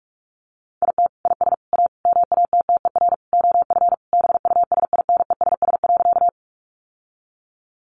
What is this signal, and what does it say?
Morse code, heard as a keyed tone, about 44 wpm, 700 Hz. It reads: IT IS A MATTER OF BUSINESS.
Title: Sentences from Top 700 Words 45wpm